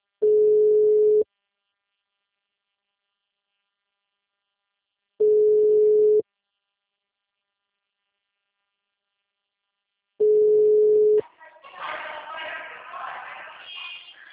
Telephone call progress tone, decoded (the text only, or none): ringback tone